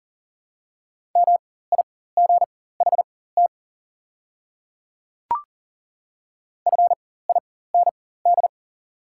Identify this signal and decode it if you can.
Morse code, heard as a keyed tone, about 40 wpm, spaced as for 19 wpm, 700 Hz. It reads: MIGHT FIND
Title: 4x Character Spacing - Top 300 Words - 40wpm